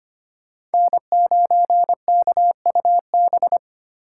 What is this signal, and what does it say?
Morse code, heard as a keyed tone, about 28 wpm, 700 Hz. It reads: N9KUB
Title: Speed Racing - USA 1x3 Call Signs 25wpm